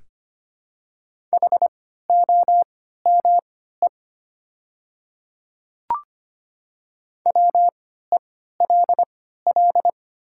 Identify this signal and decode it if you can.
Morse code, heard as a keyed tone, about 25 wpm, spaced as for 14 wpm, 700 Hz. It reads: HOME WELL
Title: 3x Character Spacing - Top 300 Words - 25wpm